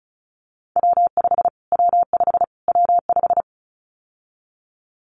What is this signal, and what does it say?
Morse code, heard as a keyed tone, about 34 wpm, 700 Hz. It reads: W5 W5 W5
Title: Callsign Prefixes for Top 25 DXCC - Repeated 3x - 35wpm